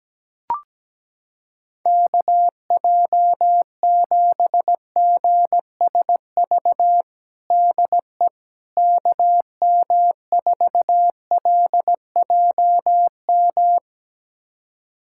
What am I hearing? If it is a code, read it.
Morse code, heard as a keyed tone, about 17 wpm, 700 Hz. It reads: KJ7GSV DE KM4LJM
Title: USA Callsign DE Callsign - 6 Characters - 17wpm